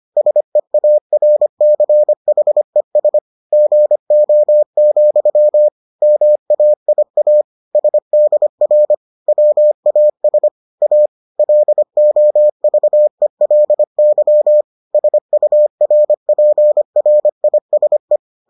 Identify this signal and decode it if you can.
Morse code, heard as a keyed tone, about 25 wpm, 600 Hz. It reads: SEARCHES GO, MAIA SDR WAS A LOVELY SURPRISE